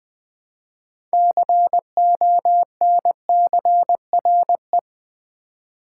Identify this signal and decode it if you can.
Morse code, heard as a keyed tone, about 20 wpm, 700 Hz. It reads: CONCRE
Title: Build A Word - 2 Letters At A Time - 8-Letter Words - 20wpm